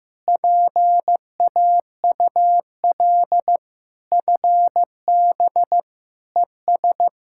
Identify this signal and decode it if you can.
Morse code, heard as a keyed tone, about 15 wpm, 700 Hz. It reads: PAUL FB ES
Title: K1USN SST QSO Practice 15wpm